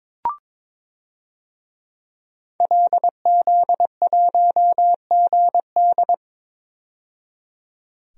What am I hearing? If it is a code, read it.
Morse code, heard as a keyed tone, about 22 wpm, 700 Hz. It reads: LZ1GD
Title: DXCC-100 Call Signs - 5 Characters - 22wpm